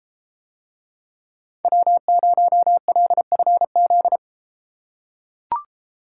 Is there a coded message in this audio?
Yes: Morse code from 1.6 to 4.2 s, heard as a keyed tone, about 31 wpm, 700 Hz. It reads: W0LFZ